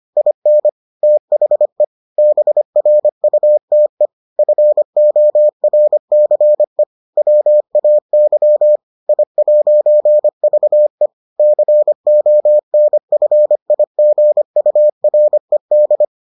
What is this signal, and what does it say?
Morse code, heard as a keyed tone, about 25 wpm, 600 Hz. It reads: IN THE BRUTE FORCE WAY I'VE CONFIGURED